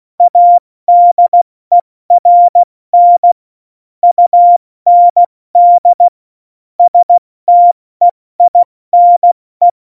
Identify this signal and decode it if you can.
Morse code, heard as a keyed tone, about 16 wpm, 700 Hz. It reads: ADERN UND STEINE
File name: Morsekurs / Witze / Witz 20. Februar 2018, 70 F80